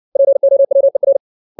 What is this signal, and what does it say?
Morse code, heard as a keyed tone, about 60 wpm, 550 Hz. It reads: POWER
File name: 632 words / 632 Words 60WPM